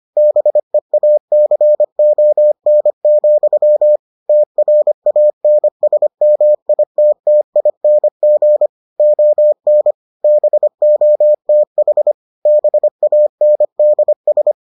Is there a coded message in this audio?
Yes: Morse code from 0.2 to 14.5 s, heard as a keyed tone, about 25 wpm, 600 Hz. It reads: BEACON, TRANSMITTING ON BOTH BANDS